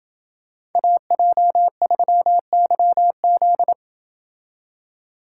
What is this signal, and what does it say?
Morse code, heard as a keyed tone, about 26 wpm, 700 Hz. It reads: AJ3YZ